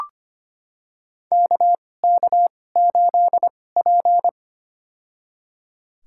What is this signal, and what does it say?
Morse code, heard as a keyed tone, about 25 wpm, spaced as for 18 wpm, 700 Hz. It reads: KK8P
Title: USA Call Signs - 4 Characters - 2x Character Spacing - 25wpm